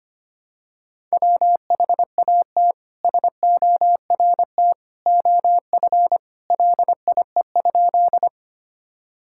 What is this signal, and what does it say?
Morse code, heard as a keyed tone, about 25 wpm, 700 Hz. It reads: WHAT SORT OF LIE?
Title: Sentences from Top 1600 Words 25wpm